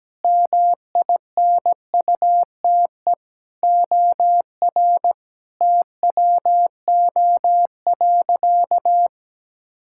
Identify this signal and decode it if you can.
Morse code, heard as a keyed tone, about 17 wpm, 700 Hz. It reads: MINUTE OR TWO.
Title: Sentences from Top 1600 Words 17wpm